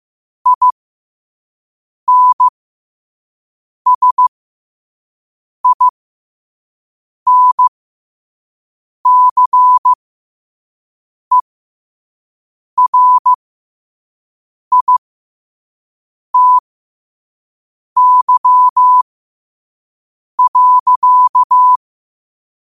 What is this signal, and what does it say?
Morse code, heard as a keyed tone, about 15 wpm, spaced as for 5 wpm, 1000 Hz. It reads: INSINCERITY.
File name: qotdfiles / 5wpm